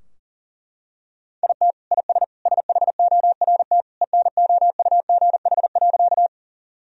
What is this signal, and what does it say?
Morse code, heard as a keyed tone, about 40 wpm, 700 Hz. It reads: IT IS SHORT ENOUGH.
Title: Sentences from Top 1000 Words - Review Part 2 - 40wpm